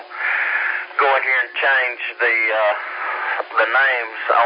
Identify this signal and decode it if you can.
Emergency Alert System (SAME) tones, heard as harsh buzzing bursts, with speech. It is heard from 0.5 to 4.0 s.